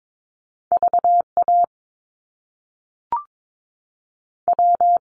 Morse code, heard as a keyed tone, about 22 wpm, 700 Hz. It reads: VA W
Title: Top Two Letter Combinations in Words 22wpm